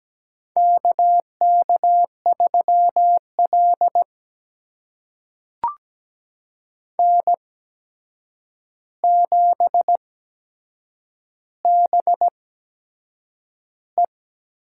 Morse code, heard as a keyed tone, about 17 wpm, spaced as for 7 wpm, 700 Hz. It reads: KK3L N7BE
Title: USA Call Signs - 4 Characters - 8x Character Spacing - 17wpm